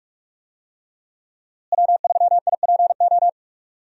Morse code, heard as a keyed tone, about 46 wpm, 700 Hz. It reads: W3IPO
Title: USA Call Signs - 5 Characters - Repeated 3x - 45wpm